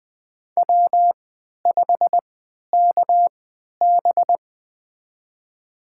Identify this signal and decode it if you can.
Morse code, heard as a keyed tone, about 20 wpm, spaced as for 11 wpm, 700 Hz. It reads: W5KB
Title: USA Call Signs - 4 Characters - 3x Character Spacing - 20wpm